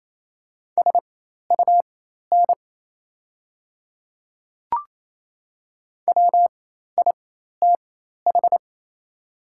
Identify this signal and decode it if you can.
Morse code, heard as a keyed tone, about 28 wpm, spaced as for 13 wpm, 700 Hz. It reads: SUN WITH
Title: 4x Character Spacing - Top 300 Words - 28wpm